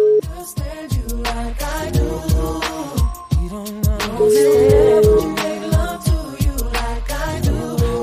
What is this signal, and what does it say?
Ringback tone (Europe, 425 Hz), with noise.